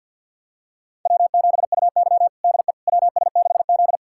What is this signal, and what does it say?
Morse code, heard as a keyed tone, about 52 wpm, 700 Hz. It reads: W7UY DE WS6Z
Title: USA Callsign DE Callsign - 4 Characters - 50wpm